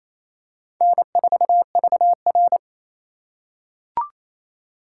Morse code, heard as a keyed tone, about 27 wpm, 700 Hz. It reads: N4VR
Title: USA Call Signs - 4 Characters - 4x Character Spacing - 28wpm